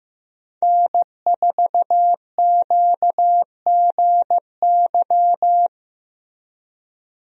Morse code, heard as a keyed tone, about 15 wpm, 700 Hz. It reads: N4QGY
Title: USA Call Signs - 5 Characters - Repeated 2x - 15wpm